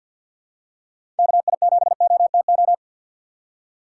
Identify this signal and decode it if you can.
Morse code, heard as a keyed tone, about 49 wpm, 700 Hz. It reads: KI7OTO
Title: USA Call Signs - 6 Characters - 4x Character Spacing - 50wpm